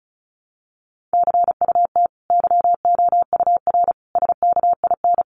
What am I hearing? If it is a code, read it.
Morse code, heard as a keyed tone, about 34 wpm, 700 Hz. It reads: CUT YOUR SKIN
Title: Sentences from Top 1300 Words 35wpm